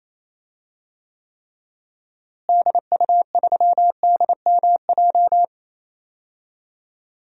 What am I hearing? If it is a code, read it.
Morse code, heard as a keyed tone, about 28 wpm, 700 Hz. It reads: DU3DMJ
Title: DXCC-100 Call Signs - 6 Characters - Repeated 2x - 28wpm